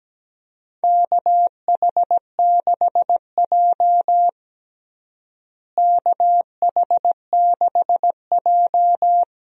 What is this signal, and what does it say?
Morse code, heard as a keyed tone, about 17 wpm, 700 Hz. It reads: KH6J KH6J
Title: DXCC-100 Call Signs - 4 Characters - Repeated 2x - 17wpm